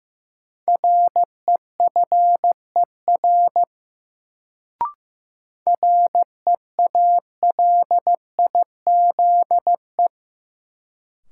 Morse code, heard as a keyed tone, about 15 wpm, 700 Hz. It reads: REFER REALIZE
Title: Sentences from Top 1000 Words - New Word Drill - 15wpm